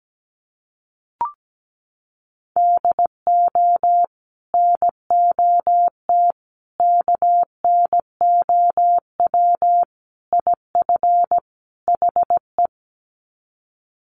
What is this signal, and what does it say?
Morse code, heard as a keyed tone, about 17 wpm, 700 Hz. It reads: DO NOT KNOW IF HE
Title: Sets of 5 Words Encore2 17wpm